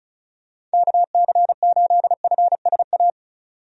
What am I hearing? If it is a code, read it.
Morse code, heard as a keyed tone, about 36 wpm, 700 Hz. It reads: KC8FSA
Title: USA Call Signs - 6 Characters - 35wpm